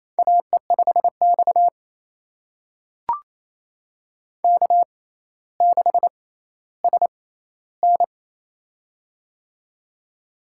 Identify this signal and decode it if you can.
Morse code, heard as a keyed tone, about 28 wpm, spaced as for 14 wpm, 700 Hz. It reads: AE5X K6SN
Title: USA Call Signs - 4 Characters - 6x Character Spacing - 28wpm